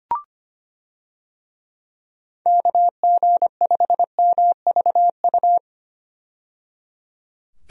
Morse code, heard as a keyed tone, about 25 wpm, 700 Hz. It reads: KG5MVU
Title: USA Call Signs - 6 Characters - 25wpm